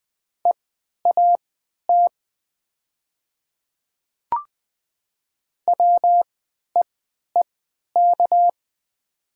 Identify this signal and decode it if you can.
Morse code, heard as a keyed tone, about 20 wpm, spaced as for 11 wpm, 700 Hz. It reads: EAT WEEK
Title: 3x Character Spacing - Top 500 Words - 20wpm